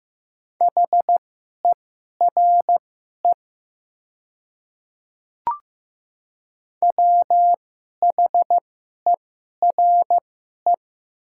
Morse code, heard as a keyed tone, about 15 wpm, spaced as for 11 wpm, 700 Hz. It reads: HERE WHERE
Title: 2x Character Spacing - Top 300 Words - 15wpm